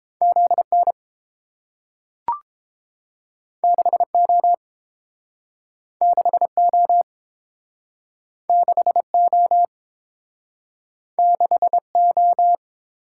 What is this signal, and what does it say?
Morse code, heard as a keyed tone, about 30 wpm, 700 Hz. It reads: ZN 6O 6O 6O 6O